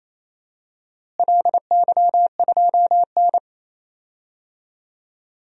Morse code, heard as a keyed tone, about 28 wpm, 700 Hz. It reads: LY2N
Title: DXCC-100 Call Signs - 4 Characters - Repeated 2x - 28wpm